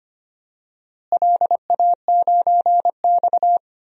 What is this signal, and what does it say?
Morse code, heard as a keyed tone, about 25 wpm, 700 Hz. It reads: LA9X
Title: DXCC-100 Call Signs - 4 Characters - Repeated 2x - 25wpm